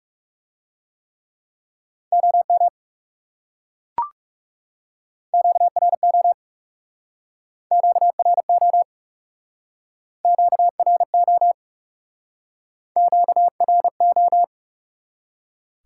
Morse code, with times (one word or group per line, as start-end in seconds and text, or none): 2.1-2.7 OM
5.3-6.3 QRO
7.7-8.8 QRO
10.2-11.5 QRO
13.0-14.4 QRO